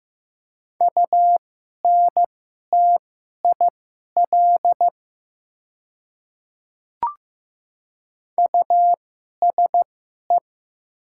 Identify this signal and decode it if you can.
Morse code, heard as a keyed tone, about 15 wpm, spaced as for 11 wpm, 700 Hz. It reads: UNTIL USE
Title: 2x Character Spacing - Top 300 Words - 15wpm